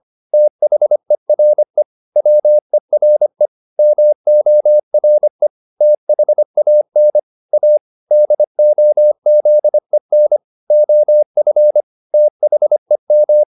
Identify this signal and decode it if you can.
Morse code, heard as a keyed tone, about 25 wpm, 600 Hz. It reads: THERE WERE MORE THAN A DOZEN OF THEM